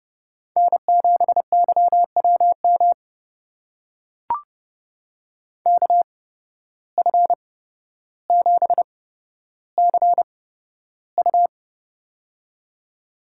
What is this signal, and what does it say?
Morse code, heard as a keyed tone, about 30 wpm, spaced as for 13 wpm, 700 Hz. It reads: N7YWM KF7CU
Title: USA Call Signs - 5 Characters - 8x Character Spacing - 30wpm